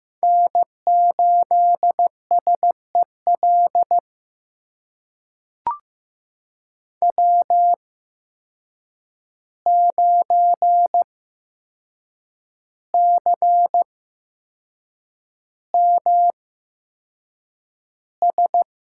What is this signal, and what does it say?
Morse code, heard as a keyed tone, about 15 wpm, spaced as for 6 wpm, 700 Hz. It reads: N8SEL W9CMS
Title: USA Call Signs - 5 Characters - 8x Character Spacing - 15wpm